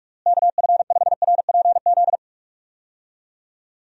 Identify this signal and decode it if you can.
Morse code, heard as a keyed tone, about 46 wpm, 700 Hz. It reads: KF5RPZ
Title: USA Call Signs - 6 Characters - Repeated 2x - 45wpm